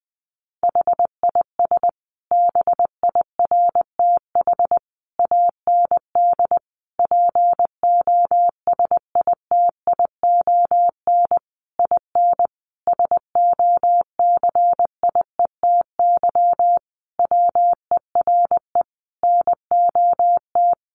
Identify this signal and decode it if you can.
Morse code, heard as a keyed tone, about 20 wpm, 700 Hz. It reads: HIS BIRTH AND POSITION IN SOCIETY WERE NOT
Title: Sentences from Top 900 Words 20wpm